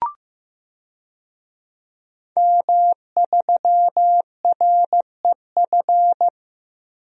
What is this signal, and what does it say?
Morse code, heard as a keyed tone, about 15 wpm, 700 Hz. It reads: M3REF
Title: DXCC-100 Call Signs - 5 Characters - 15wpm